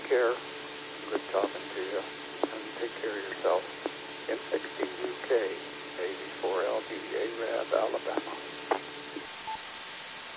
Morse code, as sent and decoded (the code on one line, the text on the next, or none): none